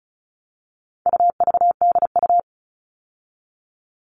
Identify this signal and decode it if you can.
Morse code, heard as a keyed tone, about 34 wpm, 700 Hz. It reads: UVDU
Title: All Cut Numbers - 4-Digit 35wpm